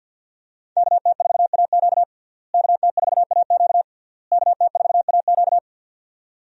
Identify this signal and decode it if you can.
Morse code, heard as a keyed tone, about 48 wpm, 700 Hz. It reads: KT4AQ KT4AQ KT4AQ
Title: USA Call Signs - 5 Characters - Repeated 3x - 50wpm